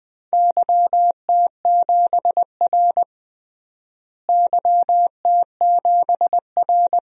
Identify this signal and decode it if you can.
Morse code, heard as a keyed tone, about 20 wpm, 700 Hz. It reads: YT7R YT7R
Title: DXCC-100 Call Signs - 4 Characters - Repeated 2x - 20wpm